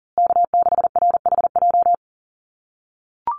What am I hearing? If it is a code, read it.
Morse code, heard as a keyed tone, about 40 wpm, 700 Hz. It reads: K6RHJ
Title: USA Call Signs Repeated 3x 40wpm